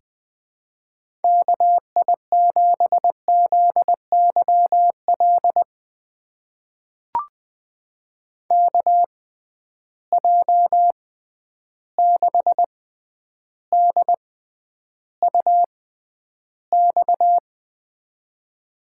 Morse code, heard as a keyed tone, about 21 wpm, spaced as for 10 wpm, 700 Hz. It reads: KI7ZYL KJ6DUX
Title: USA Call Signs - 6x Character Spacing - 20wpm